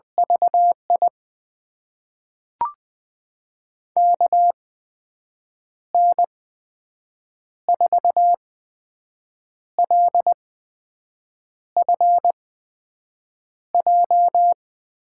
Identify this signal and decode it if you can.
Morse code, heard as a keyed tone, about 21 wpm, spaced as for 6 wpm, 700 Hz. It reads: VI KN4LFJ
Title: USA Call Signs - 6 Characters - 8x Character Spacing - 20wpm